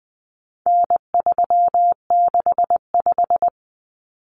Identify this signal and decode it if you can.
Morse code, heard as a keyed tone, about 20 wpm, 700 Hz. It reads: N365
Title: Common Cut Numbers - 4-Digit 20wpm